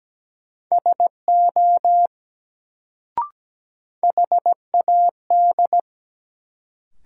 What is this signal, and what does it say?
Morse code, heard as a keyed tone, about 17 wpm, 700 Hz. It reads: SO HAD
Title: Sentences from Top 100 Words - New Word Drill - 17wpm